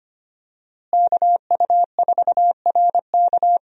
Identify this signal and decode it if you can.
Morse code, heard as a keyed tone, about 25 wpm, 700 Hz. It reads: KU4RK
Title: USA Call Signs - 5 Characters - Repeated 2x - 25wpm